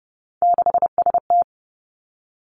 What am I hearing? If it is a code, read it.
Morse code, heard as a keyed tone, about 30 wpm, 700 Hz. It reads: 6ST
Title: Three Challenging Letter-Number Combinations 30wpm